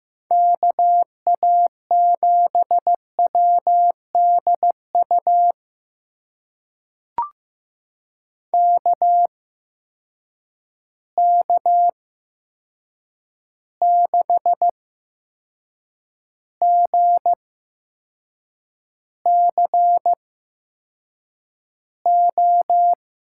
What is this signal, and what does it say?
Morse code, heard as a keyed tone, about 15 wpm, spaced as for 6 wpm, 700 Hz. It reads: KA7WDU KK6GCO
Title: USA Call Signs - 6 Characters - 8x Character Spacing - 15wpm